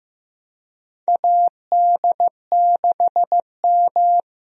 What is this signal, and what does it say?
Morse code, heard as a keyed tone, about 15 wpm, 700 Hz. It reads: AD6M
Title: USA Call Signs - 4 Characters - Repeated 2x - 15wpm